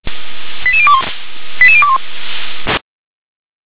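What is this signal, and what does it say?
Five-tone selective call (ZVEI), heard as quick runs of five tones, tones of about 70 ms. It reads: 88021 88021
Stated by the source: Five-tone ZVEI selective call received on SDR.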